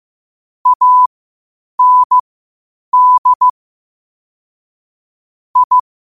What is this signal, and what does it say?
Morse code, heard as a keyed tone, about 15 wpm, spaced as for 8 wpm, 1000 Hz. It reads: AND I
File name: qotdfiles / 8wpm